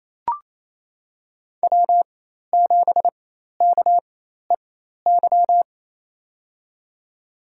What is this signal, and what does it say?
Morse code, heard as a keyed tone, about 28 wpm, spaced as for 13 wpm, 700 Hz. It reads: W7KEY
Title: USA Call Signs - 4x Character Spacing - 28wpm